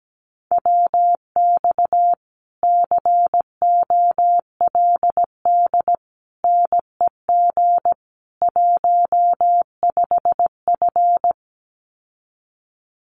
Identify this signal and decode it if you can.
Morse code, heard as a keyed tone, about 17 wpm, 700 Hz. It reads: WX COLD NEG 15F
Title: Common QSO Phrases Encore 17wpm